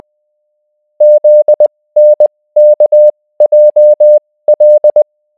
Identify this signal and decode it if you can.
Morse code, heard as a keyed tone, about 20 wpm, spaced as for 16 wpm, 600 Hz. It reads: ZNKJL